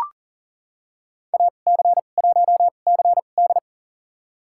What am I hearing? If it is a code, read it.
Morse code, heard as a keyed tone, about 37 wpm, spaced as for 28 wpm, 700 Hz. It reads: AC1CD
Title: USA Call Signs - 5 Characters - 2x Character Spacing - 40wpm